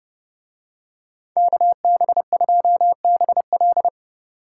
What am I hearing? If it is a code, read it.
Morse code, heard as a keyed tone, about 31 wpm, 700 Hz. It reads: KB2BL